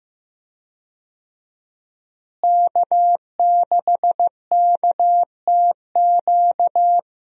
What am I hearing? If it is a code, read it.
Morse code, heard as a keyed tone, about 15 wpm, 700 Hz. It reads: K6KTQ
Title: USA Call Signs - 5 Characters - Repeated 3x - 15wpm